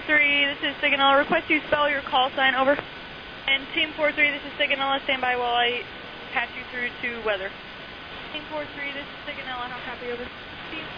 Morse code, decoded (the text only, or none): none